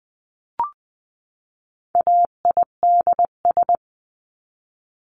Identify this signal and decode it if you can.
Morse code, heard as a keyed tone, about 20 wpm, 700 Hz. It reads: AIDS
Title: 259 Lesson 42d - Words 18-20wpm